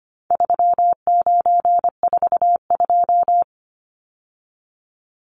Morse code, heard as a keyed tone, about 25 wpm, 700 Hz. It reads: 3942
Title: Numbers - 4-Digit 25wpm